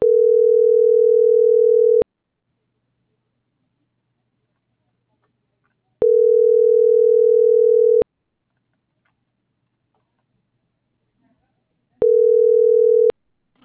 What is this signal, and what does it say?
Ringback tone (North America), 440 + 480 Hz.
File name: CENTRAL TERRESTRE / out karis - .4705